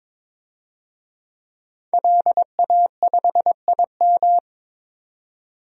Morse code, heard as a keyed tone, about 23 wpm, 700 Hz. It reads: LA5IM
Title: DXCC-100 Call Signs - 5 Characters - 22wpm